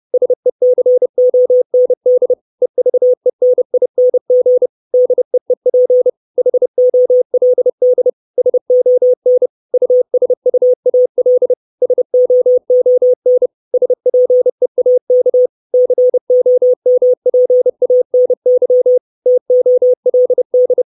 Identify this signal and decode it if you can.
Morse code, heard as a keyed tone, about 30 wpm, 500 Hz. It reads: SECOND EVENING DEEP HOLD SON USUAL SOON SPEAK COMPANY TOLD
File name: 632 Words 30WPM